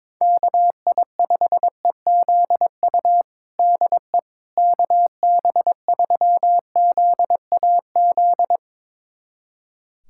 Morse code, heard as a keyed tone, about 22 wpm, 700 Hz. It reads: KI5EZU DE KB3ZAZ
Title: USA Callsign DE Callsign - 6 Characters - 22wpm